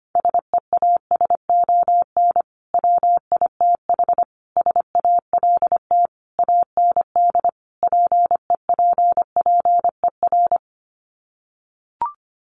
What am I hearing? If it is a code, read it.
Morse code, heard as a keyed tone, about 25 wpm, 700 Hz. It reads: SEASON WITH SALT AND PEPPER
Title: Sets of 5 Words 25wpm